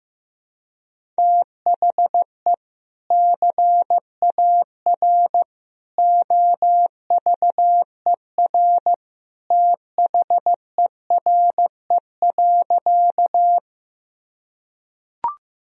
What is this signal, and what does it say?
Morse code, heard as a keyed tone, about 15 wpm, 700 Hz. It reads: THE CAR OVER THERE.
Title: Sentences from Top 1000 Words - Review Part 1 - 15wpm